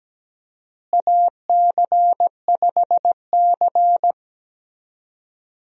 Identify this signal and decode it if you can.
Morse code, heard as a keyed tone, about 17 wpm, 700 Hz. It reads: AC5C
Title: USA Call Signs - 4 Characters - Repeated 2x - 17wpm